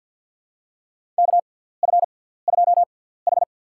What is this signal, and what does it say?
Morse code, heard as a keyed tone, about 47 wpm, spaced as for 17 wpm, 700 Hz. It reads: KF2H
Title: USA Call Signs - 6x Character Spacing - 50wpm